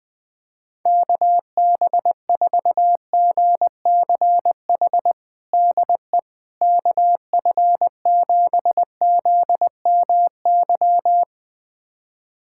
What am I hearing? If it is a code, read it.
Morse code, heard as a keyed tone, about 21 wpm, 700 Hz. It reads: KB4GCH DE KF7ZMY
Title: USA Callsign DE Callsign 20wpm